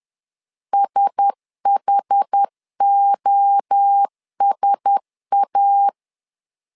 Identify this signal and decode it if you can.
Morse code, heard as a keyed tone, about 10 wpm, 800 Hz. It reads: SHOSA
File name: LEZIONI / 4 Q1S 10wpm 2133